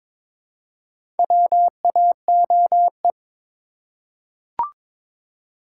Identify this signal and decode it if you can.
Morse code, heard as a keyed tone, about 22 wpm, 700 Hz. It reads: WAOE
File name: practice / 22wpm / Four Letter-Number Combinations - 22wpm